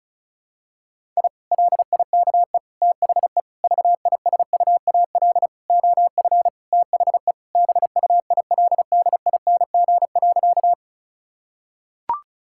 Morse code, heard as a keyed tone, about 34 wpm, 700 Hz. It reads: I LIKE THE VISUAL OF THE BUILDING.
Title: Sentences from Top 1000 Words - Review Part 6 - 35wpm